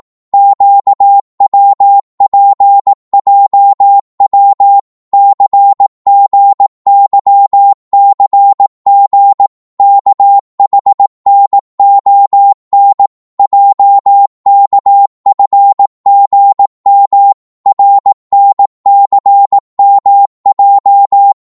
Morse code, heard as a keyed tone, about 18 wpm, 800 Hz. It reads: QWPJW CGYCG KHNON JKFGM RNCMJ
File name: Morse / ltr-18wpm